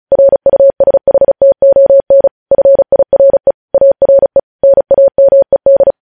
Morse code, heard as a keyed tone, about 36 wpm, 550 Hz. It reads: RUSHTON FIRE ARE NAMED
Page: News headlines in Morse code at 35 wpm